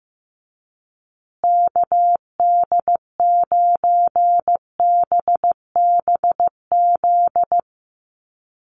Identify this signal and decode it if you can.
Morse code, heard as a keyed tone, about 15 wpm, 700 Hz. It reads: KD9BBZ
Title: USA Call Signs Repeated 3x 15wpm